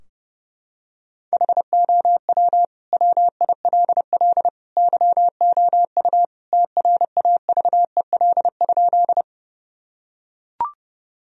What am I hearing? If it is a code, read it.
Morse code, heard as a keyed tone, about 30 wpm, 700 Hz. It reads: HOW WILL YOU TRAVEL?
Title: Sentences from Top 1000 Words - Review Part 2 - 30wpm